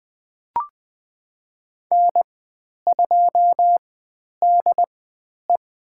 Morse code, heard as a keyed tone, about 20 wpm, spaced as for 10 wpm, 700 Hz. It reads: N2DE
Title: 095 Lesson 16g - Letter c - Call Signs 20wpm